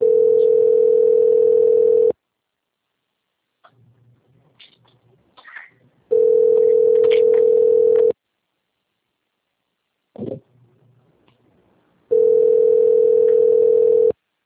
Ringback tone (North America), 440 + 480 Hz.
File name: out- - - - - .930